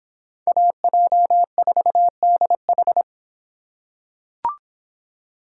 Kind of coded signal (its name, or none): Morse code